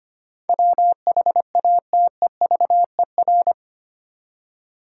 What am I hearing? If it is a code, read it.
Morse code, heard as a keyed tone, about 25 wpm, 700 Hz. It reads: WHATEVER
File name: Eight Letter Words - 25wpm